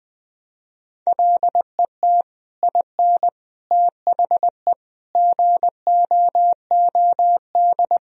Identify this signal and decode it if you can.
Morse code, heard as a keyed tone, about 20 wpm, 700 Hz. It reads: LET IN THE GOOD
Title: Sentences from Top 1000 Words - Review Part 1 - 20wpm